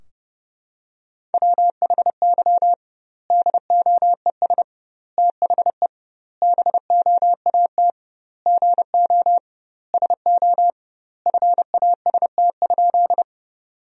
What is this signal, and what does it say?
Morse code, heard as a keyed tone, about 30 wpm, 700 Hz. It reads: WHY DOES THE BOAT GO SO FAST?
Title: 2x Word-Spacing - Sentences from Top 500 Words - 30wpm